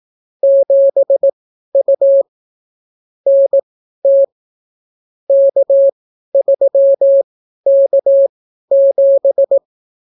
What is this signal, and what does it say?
Morse code, heard as a keyed tone, about 18 wpm, spaced as for 12 wpm, 550 Hz. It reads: 7U NT K3K7